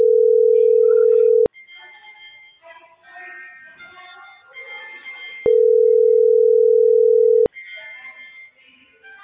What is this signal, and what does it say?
Ringback tone (North America), 440 + 480 Hz.